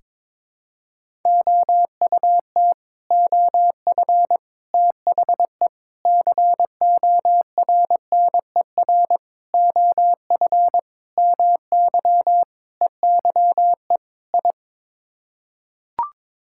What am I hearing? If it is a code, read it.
Morse code, heard as a keyed tone, about 22 wpm, 700 Hz. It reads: OUT OF THE CORNER OF MY EYE I
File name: practice / 22wpm / Sets of 8 Words - 22wpm